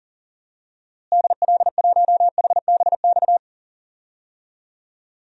Morse code, heard as a keyed tone, about 40 wpm, 700 Hz. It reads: DL1HBX
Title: DXCC-100 Call Signs - 6 Characters - Repeated 2x - 40wpm